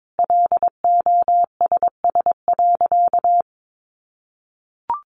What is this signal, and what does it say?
Morse code, heard as a keyed tone, about 22 wpm, 700 Hz. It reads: LOSS.
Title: Sentences from Top 1400 Words 22wpm